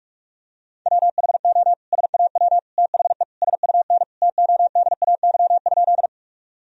Morse code, heard as a keyed tone, about 46 wpm, 700 Hz. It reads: WHO SAW THE SUN TODAY?
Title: Sentences from Top 1000 Words - Review Part 1 - 45wpm